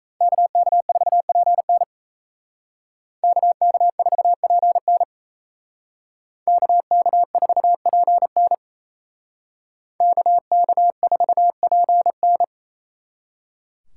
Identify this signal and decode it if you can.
Morse code, heard as a keyed tone, about 36 wpm, 700 Hz. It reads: KK4PN KK4PN KK4PN KK4PN